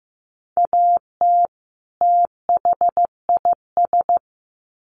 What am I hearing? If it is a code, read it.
Morse code, heard as a keyed tone, about 15 wpm, 700 Hz. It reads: AT THIS
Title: Sets of 2 Words 15wpm